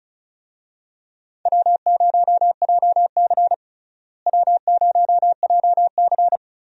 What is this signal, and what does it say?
Morse code, heard as a keyed tone, about 36 wpm, 700 Hz. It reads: W0JC W0JC
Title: USA Call Signs - 4 Characters - Repeated 3x - 35wpm